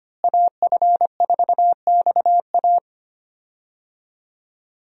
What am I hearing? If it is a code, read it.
Morse code, heard as a keyed tone, about 25 wpm, 700 Hz. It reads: AF4XA
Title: USA Call Signs - 5 Characters - Repeated 2x - 25wpm